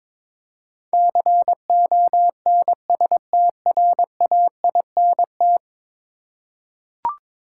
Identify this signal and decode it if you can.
Morse code, heard as a keyed tone, about 22 wpm, 700 Hz. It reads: CONSTRAINT
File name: practice / 22wpm / 10 Letter Words - First 3 Letters Given - 22wpm